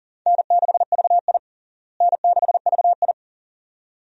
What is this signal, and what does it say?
Morse code, heard as a keyed tone, about 40 wpm, 700 Hz. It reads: N6VI N6VI
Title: USA Call Signs - 4 Characters - Repeated 2x - 40wpm